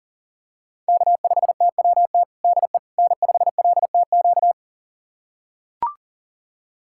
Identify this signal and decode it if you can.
Morse code, heard as a keyed tone, about 40 wpm, 700 Hz. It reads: K5TWT DE N5LTQ
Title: USA Callsign DE Callsign - 5 Characters - 40wpm